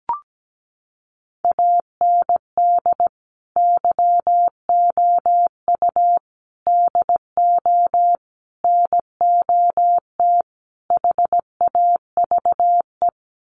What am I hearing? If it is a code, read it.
Morse code, heard as a keyed tone, about 17 wpm, 690 Hz. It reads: AND YOU DO NOT HAVE
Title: Sets of 5 Words Encore 17wpm